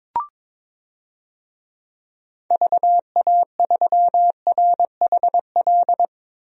Morse code, heard as a keyed tone, about 21 wpm, 700 Hz. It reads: VA3RHL
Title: DXCC-100 Call Signs - 6 Characters - 22wpm